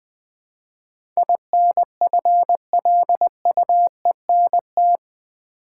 Morse code, heard as a keyed tone, about 20 wpm, 700 Hz. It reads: INFLUENT
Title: Build A Word - 4 Letters At A Time - 11-Letter Words - 20wpm